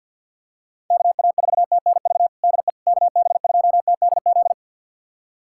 Morse code, heard as a keyed tone, about 49 wpm, 700 Hz. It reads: KA4TNV DE KB2TDZ
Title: USA Callsign DE Callsign - 6 Characters - 50wpm